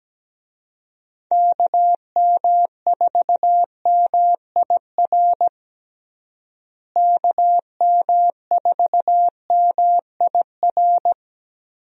Morse code, heard as a keyed tone, about 17 wpm, 700 Hz. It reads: KM4MIR KM4MIR
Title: USA Call Signs - 6 Characters - Repeated 2x - 17wpm